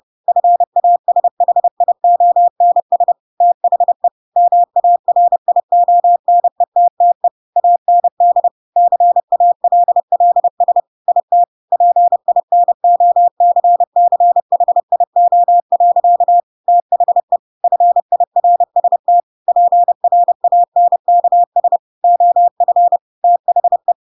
Morse code, heard as a keyed tone, about 30 wpm, 700 Hz. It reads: FASHIONS THE MARIONETTE AND CALLS IT PINOCCHIO. THE FIRST PRANKS OF THE
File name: Pinocchio / Pinocchio ch3 30 30 ik1hge